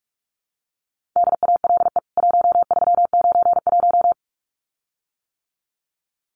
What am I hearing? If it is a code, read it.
Morse code, heard as a keyed tone, about 44 wpm, 700 Hz. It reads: DALE 1391
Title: CWT Exchange 45wpm